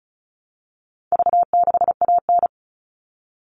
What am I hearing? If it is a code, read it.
Morse code, heard as a keyed tone, about 34 wpm, 700 Hz. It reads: V6AN
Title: All Cut Numbers - 4-Digit 35wpm